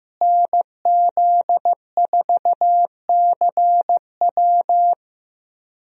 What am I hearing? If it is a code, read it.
Morse code, heard as a keyed tone, about 15 wpm, 700 Hz. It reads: NZ4CW
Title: USA Call Signs - 5 Characters - Repeated 2x - 15wpm